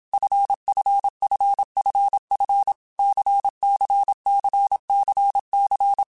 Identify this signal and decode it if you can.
Morse code, heard as a keyed tone, about 27 wpm, 800 Hz. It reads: FFFFF CCCCC
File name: Basic Letter Groups By7Groups / LFCYQ / LFCYQ@25